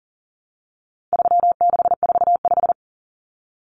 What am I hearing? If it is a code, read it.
Morse code, heard as a keyed tone, about 40 wpm, 700 Hz. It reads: 3645
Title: Common Cut Numbers - 4-Digit 40wpm